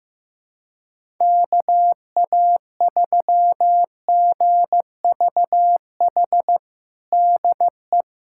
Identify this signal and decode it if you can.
Morse code, heard as a keyed tone, about 15 wpm, 700 Hz. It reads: KA3GVH DE
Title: USA Callsign DE Callsign - 6 Characters - 15wpm